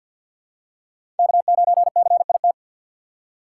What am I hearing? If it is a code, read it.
Morse code, heard as a keyed tone, about 50 wpm, 690 Hz. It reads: K9CIT
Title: USA Call Signs - 5 Characters - 3x Character Spacing - 50wpm